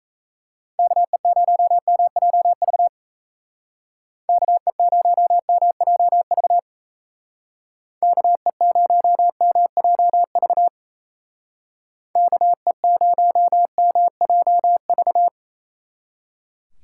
Morse code, heard as a keyed tone, about 37 wpm, 700 Hz. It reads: KE0MJV KE0MJV KE0MJV KE0MJV